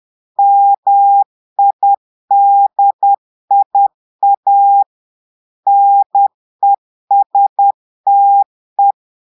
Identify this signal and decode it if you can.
Morse code, heard as a keyed tone, about 10 wpm, 800 Hz. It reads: MIDIA NESTE
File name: Portugais - texte en clair 1 - vitesse - 10 mots minute